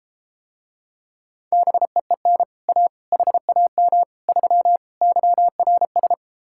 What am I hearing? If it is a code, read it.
Morse code, heard as a keyed tone, about 32 wpm, 700 Hz. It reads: BEEN A HAM 3 YRS